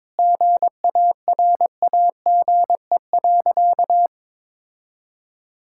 Morse code, heard as a keyed tone, about 22 wpm, 700 Hz. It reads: GARAGE.
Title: Sentences from Top 1600 Words 22wpm